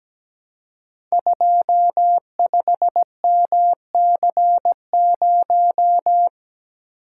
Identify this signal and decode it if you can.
Morse code, heard as a keyed tone, about 17 wpm, 700 Hz. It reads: 25MC0
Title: Code Groups - 5 Letter:Numbers - 17wpm